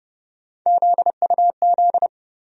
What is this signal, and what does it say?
Morse code, heard as a keyed tone, about 32 wpm, 700 Hz. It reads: ZUZ
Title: USA Call Sign Postfixes - 3 Letters 30wpm